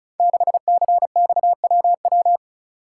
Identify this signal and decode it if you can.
Morse code, heard as a keyed tone, about 35 wpm, 700 Hz. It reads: 6CXWW
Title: Code Groups - 5 Letter:Numbers - 35wpm